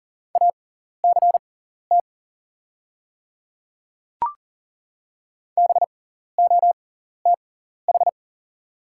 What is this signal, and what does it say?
Morse code, heard as a keyed tone, about 40 wpm, spaced as for 14 wpm, 700 Hz. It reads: ACT BOTH
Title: 6x Character Spacing - Top 300 Words - 40wpm